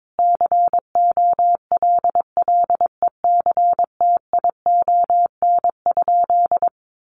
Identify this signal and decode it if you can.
Morse code, heard as a keyed tone, about 22 wpm, 700 Hz. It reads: COLLECTION?
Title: Sentences from Top 1200 Words 22wpm